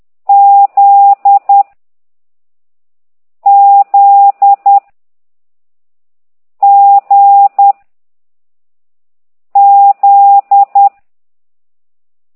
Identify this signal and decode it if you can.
Morse code, heard as a keyed tone, about 10 wpm, 800 Hz. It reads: Z Z G Z
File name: ct2idw / exerc15